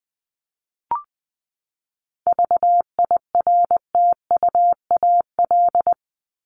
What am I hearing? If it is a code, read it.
Morse code, heard as a keyed tone, about 20 wpm, 700 Hz. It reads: VIRTUAL
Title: LY Suffix - Top 100 Words with Base Words 20wpm